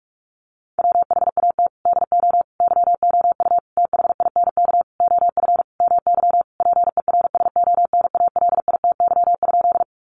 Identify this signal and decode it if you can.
Morse code, heard as a keyed tone, about 44 wpm, 700 Hz. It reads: WHAT DO YOU THINK OF MY PERSONALITY?
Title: Sentences from Top 1400 Words 45wpm